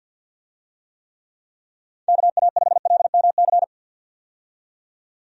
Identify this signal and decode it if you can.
Morse code, heard as a keyed tone, about 51 wpm, 700 Hz. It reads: KA5LMC
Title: USA Call Signs - 6 Characters - 50wpm